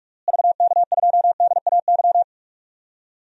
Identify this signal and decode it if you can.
Morse code, heard as a keyed tone, about 41 wpm, 700 Hz. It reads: VK2DAY
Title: DXCC-100 Call Signs - 6 Characters - Repeated 3x - 45wpm